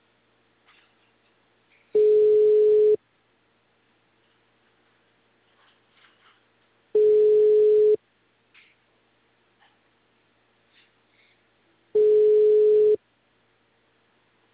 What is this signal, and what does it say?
Ringback tone (Europe, 425 Hz).
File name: out- -2003- - - .0